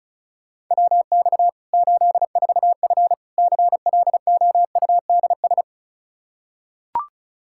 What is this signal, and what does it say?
Morse code, heard as a keyed tone, about 34 wpm, 700 Hz. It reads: WX 84F CLOUDS
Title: Practice QSO - Exchange 2 - Info - 35wpm